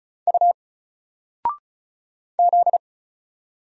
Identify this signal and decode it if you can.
Morse code, heard as a keyed tone, about 35 wpm, 700 Hz. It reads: U Z
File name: Single-Letter-Number-35wpm